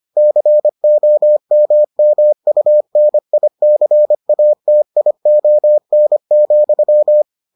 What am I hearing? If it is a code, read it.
Morse code, heard as a keyed tone, about 25 wpm, 600 Hz. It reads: COMMUNICATION,